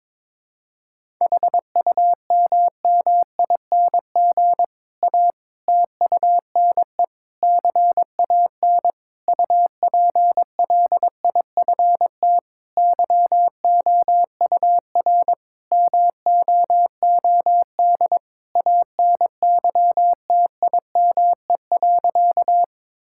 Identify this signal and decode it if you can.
Morse code, heard as a keyed tone, about 22 wpm, 700 Hz. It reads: HUMMING A TUNE CAN UPLIFT YOUR MOOD ANYTIME.
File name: Difficult Sentences with Hints - 22wpm